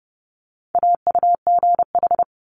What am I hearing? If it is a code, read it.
Morse code, heard as a keyed tone, about 30 wpm, 700 Hz. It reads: AUGH
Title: Multi-letter Phonemes Quick Fire 30wpm